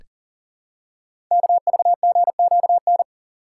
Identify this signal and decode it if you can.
Morse code, heard as a keyed tone, about 38 wpm, 700 Hz. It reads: KVGQN
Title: Code Groups - 5 Letters - 40wpm